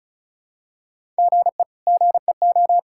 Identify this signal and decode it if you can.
Morse code, heard as a keyed tone, about 34 wpm, 700 Hz. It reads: GE GEO
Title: K1USN SST QSO Practice - 35wpm